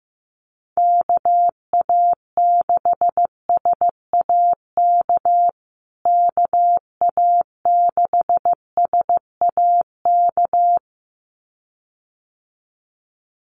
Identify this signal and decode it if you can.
Morse code, heard as a keyed tone, about 15 wpm, 700 Hz. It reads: KA6SAK KA6SAK
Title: USA Call Signs Repeated 2x 15wpm